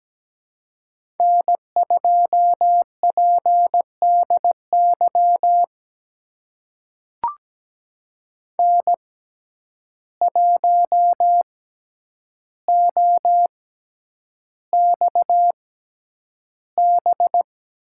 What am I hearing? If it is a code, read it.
Morse code, heard as a keyed tone, about 17 wpm, spaced as for 9 wpm, 700 Hz. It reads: N2PDY N1OXB
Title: USA Call Signs - 5 Characters - 6x Character Spacing - 17wpm